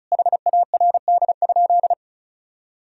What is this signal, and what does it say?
Morse code, heard as a keyed tone, about 34 wpm, 700 Hz. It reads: HARD?
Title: Sentences from Top 1000 Words - Review Part 4 - 35wpm